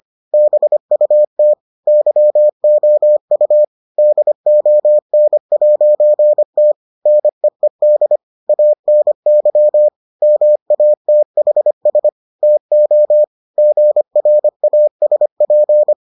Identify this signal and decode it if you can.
Morse code, heard as a keyed tone, about 25 wpm, 600 Hz. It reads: BUT YOU DON'T NEED ANY MATHS TO GRASP